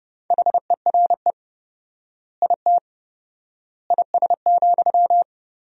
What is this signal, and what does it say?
Morse code, heard as a keyed tone, about 30 wpm, 700 Hz. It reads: HERE IT IS,
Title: 4x Word-Spacing - Sentences from Top 200 Words - 30wpm